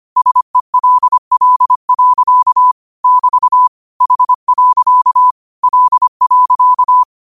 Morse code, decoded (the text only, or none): SELL. <BT> H. L.